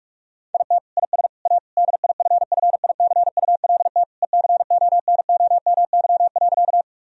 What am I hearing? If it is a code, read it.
Morse code, heard as a keyed tone, about 46 wpm, 700 Hz. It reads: IT IS A DIFFICULT ECONOMY.
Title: Sentences from Top 1000 Words - Review Part 3 - 45wpm